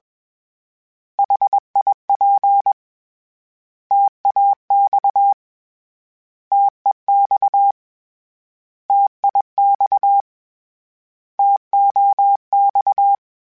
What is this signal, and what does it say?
Morse code, heard as a keyed tone, about 21 wpm, 800 Hz. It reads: HIP TAX TEX TIX TOX
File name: Contrasts / Contrasts-mp3s / 3LtrContrasts @20